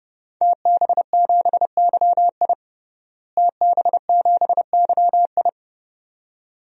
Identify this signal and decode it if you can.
Morse code, heard as a keyed tone, about 32 wpm, 700 Hz. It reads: TB7YI TB7YI
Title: DXCC-100 Call Signs - 5 Characters - Repeated 2x - 30wpm